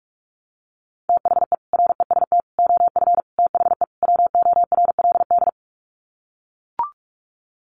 Morse code, heard as a keyed tone, about 44 wpm, 700 Hz. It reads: THE REST OF THE WORLD
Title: Sets of 5 Words 45wpm